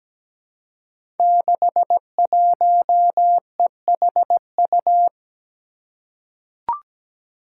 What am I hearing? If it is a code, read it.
Morse code, heard as a keyed tone, about 17 wpm, 700 Hz. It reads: 61EHU
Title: Five Challenging Letter-Number Combinations 17wpm